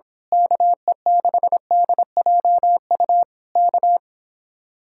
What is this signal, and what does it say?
Morse code, heard as a keyed tone, about 26 wpm, 700 Hz. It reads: KE6DJU K
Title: Speed-Racing Common QSO Phrases - 22wpm